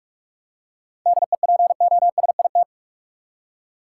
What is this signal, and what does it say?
Morse code, heard as a keyed tone, about 46 wpm, 700 Hz. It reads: DEPOSIT
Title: Seven Letter Words - 45wpm